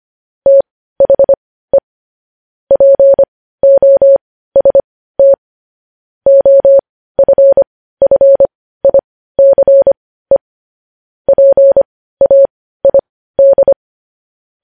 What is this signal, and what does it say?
Morse code, heard as a keyed tone, about 25 wpm, spaced as for 15 wpm, 550 Hz. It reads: THE POST OFFICE PAID
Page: news-headlines-in-morse-code-at-15-wpm